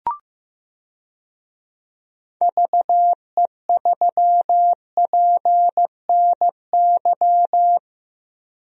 Morse code, heard as a keyed tone, about 15 wpm, 700 Hz. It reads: VE3PNY
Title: DXCC-100 Call Signs - 6 Characters - 15wpm